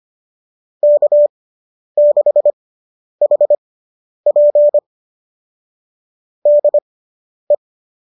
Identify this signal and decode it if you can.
Morse code, heard as a keyed tone, about 25 wpm, spaced as for 10 wpm, 590 Hz. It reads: K6HP DE